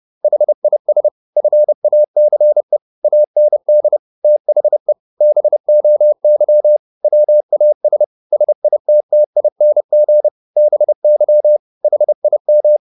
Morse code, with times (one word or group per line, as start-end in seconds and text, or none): 0.2-1.1 HIS
1.4-2.8 FACE
3.0-4.0 AND
4.2-4.9 THE
5.2-6.8 BOY
7.0-8.0 WAS
8.3-10.3 SITTING
10.6-11.6 BY
11.8-12.8 HIM